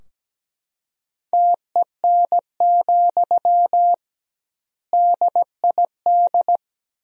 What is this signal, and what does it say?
Morse code, heard as a keyed tone, about 17 wpm, 700 Hz. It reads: TEN, DID
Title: 2x Word-Spacing - Sentences from Top 400 Words - 17wpm